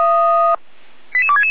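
Five-tone selective call (ZVEI), heard as quick runs of five tones, tones of about 70 ms. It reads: 88266
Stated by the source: Five-tone ZVEI selective call received on an SDR.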